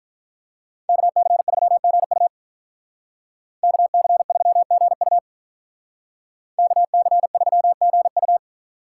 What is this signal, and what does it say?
Morse code, heard as a keyed tone, about 44 wpm, 700 Hz. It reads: KC3GU KC3GU KC3GU